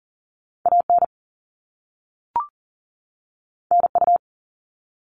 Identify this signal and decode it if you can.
Morse code, heard as a keyed tone, about 40 wpm, 700 Hz. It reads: AN NU
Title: All Cut Numbers - 2-Digit 40wpm